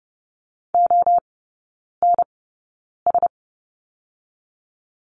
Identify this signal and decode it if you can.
Morse code, heard as a keyed tone, about 30 wpm, spaced as for 9 wpm, 700 Hz. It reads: ONS
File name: Lesson 16e - Letter c - Three Characters 30wpm